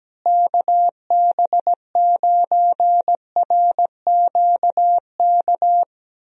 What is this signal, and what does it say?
Morse code, heard as a keyed tone, about 17 wpm, 700 Hz. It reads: KB9RQK
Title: USA Call Signs - 6 Characters - Repeated 2x - 17wpm